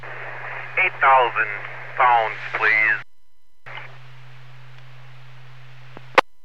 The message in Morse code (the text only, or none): none